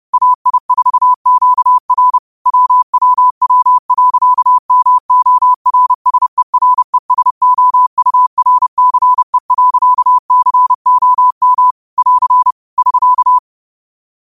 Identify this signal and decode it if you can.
Morse code, heard as a keyed tone, about 30 wpm, 1000 Hz. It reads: AI4QR WWW.MORSERESOURCE.COM <AR> <SK>
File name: qotdfiles / 30wpm